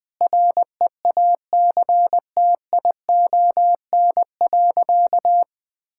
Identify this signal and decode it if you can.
Morse code, heard as a keyed tone, about 20 wpm, 700 Hz. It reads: REACTION.
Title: Sentences with 8-Word Phrases - 20wpm